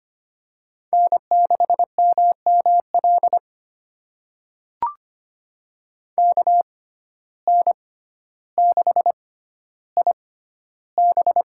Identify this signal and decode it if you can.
Morse code, heard as a keyed tone, about 25 wpm, spaced as for 13 wpm, 700 Hz. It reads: N6MML KN6IB
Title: USA Call Signs - 5 Characters - 6x Character Spacing - 25wpm